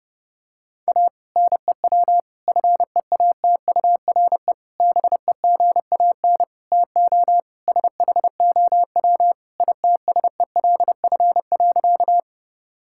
Morse code, heard as a keyed tone, about 30 wpm, 700 Hz. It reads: A NEW FEATURE BEGAN TO SHOW ITSELF.
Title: Sentences from Top 1000 Words - Review Part 5 - 30wpm